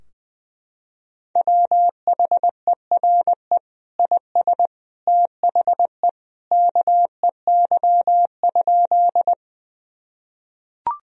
Morse code, heard as a keyed tone, about 20 wpm, 700 Hz. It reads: WHERE IS THE KEY?
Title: Sentences from Top 1000 Words - Review Part 4 - 20wpm